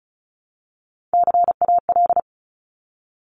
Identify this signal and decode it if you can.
Morse code, heard as a keyed tone, about 34 wpm, 700 Hz. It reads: CAL
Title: Top Three Letter Combinations in Words 35wpm